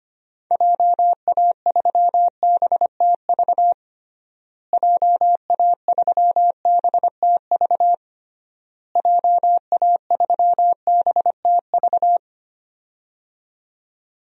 Morse code, heard as a keyed tone, about 25 wpm, 700 Hz. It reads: JA3BTV JA3BTV JA3BTV
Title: DXCC-100 Call Signs - 6 Characters - Repeated 3x - 25wpm